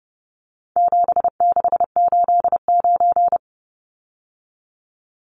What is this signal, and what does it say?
Morse code, heard as a keyed tone, about 30 wpm, 700 Hz. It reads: 7689
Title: Numbers - 4-Digit 30wpm